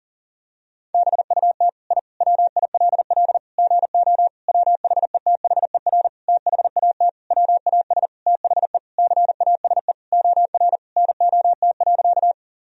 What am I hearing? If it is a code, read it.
Morse code, heard as a keyed tone, about 37 wpm, 700 Hz. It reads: BUT I WILL GO WHETHER THAT WAS THE CASE OR NOT.
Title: Sentences from Top 1000 Words - Review Part 3 - 40wpm